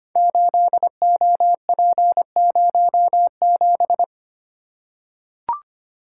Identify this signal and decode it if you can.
Morse code, heard as a keyed tone, about 25 wpm, 700 Hz. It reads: 8OP07
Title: Code Groups - 5 Letter:Numbers - 25wpm